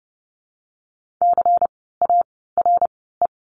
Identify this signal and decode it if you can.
Morse code, heard as a keyed tone, about 32 wpm, 700 Hz. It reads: C A R E
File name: Lesson 16f - Letter c - Words 30wpm